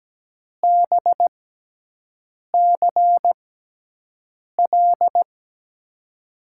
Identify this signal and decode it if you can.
Morse code, heard as a keyed tone, about 17 wpm, spaced as for 6 wpm, 700 Hz. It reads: BCL
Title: USA Call Signs - 6 Characters - 6x Character Spacing - 17wpm